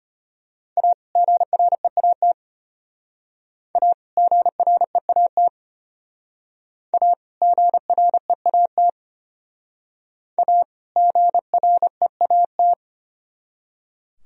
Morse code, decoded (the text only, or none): A GREAT A GREAT A GREAT A GREAT